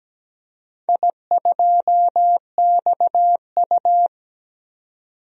Morse code, heard as a keyed tone, about 17 wpm, 700 Hz. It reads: I2XU
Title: DXCC-100 Call Signs - 4 Characters - 17wpm